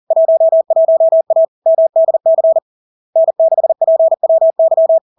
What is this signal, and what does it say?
Morse code, heard as a keyed tone, about 40 wpm, 650 Hz. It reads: 11A MDC N6PWY
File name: LARK-FD-2022-Log-in-CW-40wpm-650-Hz